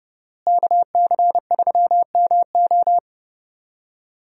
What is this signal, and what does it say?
Morse code, heard as a keyed tone, about 30 wpm, 700 Hz. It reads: KC3MO
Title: USA Call Signs - 5 Characters - Repeated 3x - 30wpm